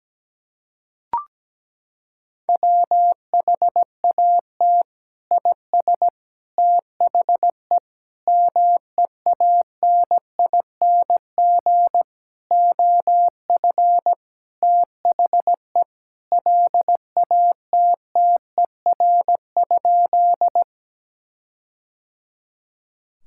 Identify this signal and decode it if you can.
Morse code, heard as a keyed tone, about 17 wpm, 700 Hz. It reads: WHAT IS THE MEANING OF THE LATTER?
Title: Sentences from Top 1600 Words 17wpm